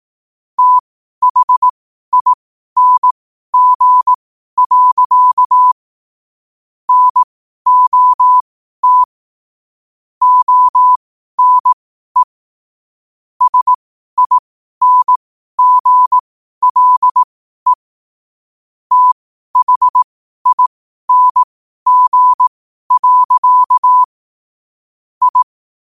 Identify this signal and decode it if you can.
Morse code, heard as a keyed tone, about 18 wpm, spaced as for 12 wpm, 1000 Hz. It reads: THING. NOT ONE SINGLE THING. I
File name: qotdfiles / 12wpm